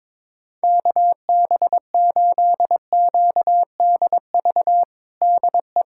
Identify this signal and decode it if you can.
Morse code, heard as a keyed tone, about 22 wpm, 700 Hz. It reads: KB8QDV DE
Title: USA Callsign DE Callsign - 6 Characters - 22wpm